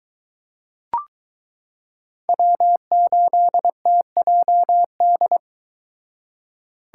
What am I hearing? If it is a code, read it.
Morse code, heard as a keyed tone, about 21 wpm, 700 Hz. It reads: W8TJD